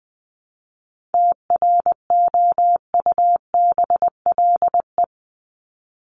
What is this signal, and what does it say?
Morse code, heard as a keyed tone, about 20 wpm, 700 Hz. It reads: TROUBLE
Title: Top 1001-1250 Words 20wpm